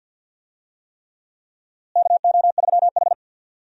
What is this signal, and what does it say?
Morse code, heard as a keyed tone, about 46 wpm, 700 Hz. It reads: KO3H
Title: USA Call Signs - 4 Characters - Repeated 3x - 50wpm